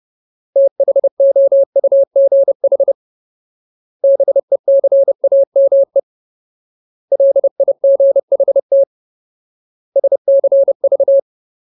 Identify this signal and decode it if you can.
Morse code, heard as a keyed tone, about 30 wpm, 550 Hz. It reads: THOUGH BECAME LIGHT SCV
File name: Ultimate Word List / 30WPM Ultimate